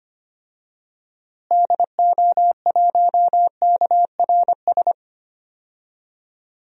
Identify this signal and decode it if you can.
Morse code, heard as a keyed tone, about 25 wpm, 700 Hz. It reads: DO1KRS
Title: DXCC-100 Call Signs - 6 Characters - Repeated 3x - 25wpm